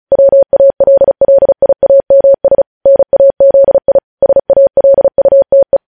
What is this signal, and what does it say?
Morse code, heard as a keyed tone, about 36 wpm, 550 Hz. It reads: WALLIAMS NAZI SALUTE
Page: news headlines in morse code at 35 wpm